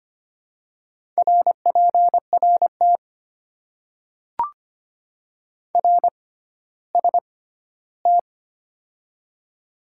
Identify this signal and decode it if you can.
Morse code, heard as a keyed tone, about 25 wpm, spaced as for 14 wpm, 700 Hz. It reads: RPRT RST
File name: Top 90 QSO Elements - 6x Character Spacing - 25wpm